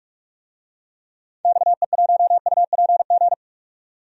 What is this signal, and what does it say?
Morse code, heard as a keyed tone, about 45 wpm, 700 Hz. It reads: XE1UPG
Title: DXCC-100 Call Signs - 6 Characters - 45wpm